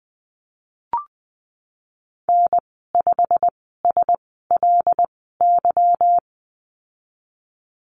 Morse code, heard as a keyed tone, about 20 wpm, spaced as for 14 wpm, 700 Hz. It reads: N5SLY
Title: 161 Lesson 26g - Letter g - Call Signs 20wpm